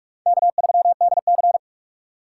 Morse code, heard as a keyed tone, about 42 wpm, 700 Hz. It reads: K3DC
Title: USA Call Signs - 4 Characters - 6x Character Spacing - 45wpm